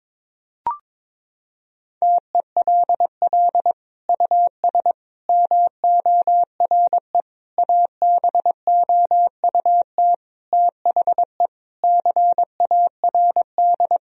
Morse code, heard as a keyed tone, about 22 wpm, 700 Hz. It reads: TELL US MORE ABOUT THE CARD
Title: Sentences from Top 1000 Words - Review Part 4 - 22wpm